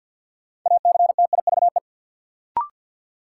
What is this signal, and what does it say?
Morse code, heard as a keyed tone, about 46 wpm, 700 Hz. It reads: ACTIVE
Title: Sentences from Top 900 Words - New Word Drill - 50wpm